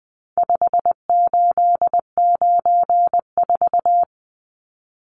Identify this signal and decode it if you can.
Morse code, heard as a keyed tone, about 20 wpm, 700 Hz. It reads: 5894
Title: Numbers - 4-Digit 20wpm